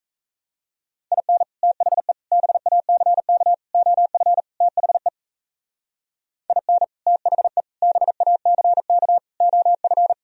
Morse code, heard as a keyed tone, about 35 wpm, 700 Hz. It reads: IN THE BACK OF THE IN THE BACK OF
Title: Speed-Racing - Sets of 5 Words 28wpm